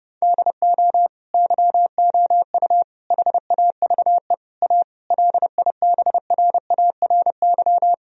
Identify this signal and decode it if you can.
Morse code, heard as a keyed tone, about 30 wpm, 700 Hz. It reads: DO YOU HAVE A LIBRARY
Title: Sentences from Top 1000 Words - Review Part 4 - 30wpm